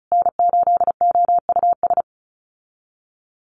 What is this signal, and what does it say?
Morse code, heard as a keyed tone, about 34 wpm, 700 Hz. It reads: N8OUS
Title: USA Call Signs Repeated 3x 35wpm